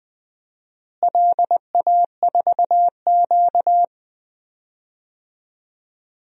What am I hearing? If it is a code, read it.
Morse code, heard as a keyed tone, about 20 wpm, 700 Hz. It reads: LA4Q
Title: DXCC-100 Call Signs - 4 Characters - Repeated 2x - 20wpm